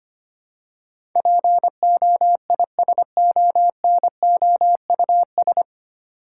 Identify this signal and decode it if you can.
Morse code, heard as a keyed tone, about 25 wpm, 700 Hz. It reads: POISONOUS
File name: Nine Letter Words - 25wpm